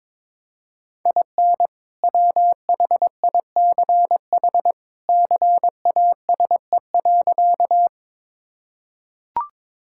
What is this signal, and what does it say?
Morse code, heard as a keyed tone, about 22 wpm, 700 Hz. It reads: IN WHICH CASE.
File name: practice / 22wpm / Sentences with 2-Word Phrases - 22wpm